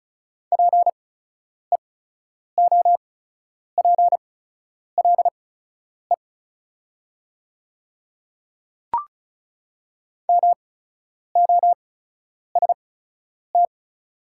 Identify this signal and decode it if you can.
Morse code, heard as a keyed tone, about 36 wpm, spaced as for 10 wpm, 700 Hz. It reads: PEOPLE MOST
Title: 8x Character Spacing - Top 100 Words - 35wpm